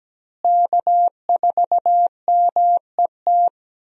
Morse code, heard as a keyed tone, about 17 wpm, 700 Hz. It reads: K4MET
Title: USA Call Signs - 5 Characters - Repeated 2x - 17wpm